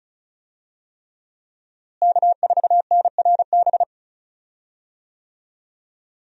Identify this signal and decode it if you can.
Morse code, heard as a keyed tone, about 36 wpm, 700 Hz. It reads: K4NRB
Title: USA Call Signs - 5 Characters - Repeated 3x - 35wpm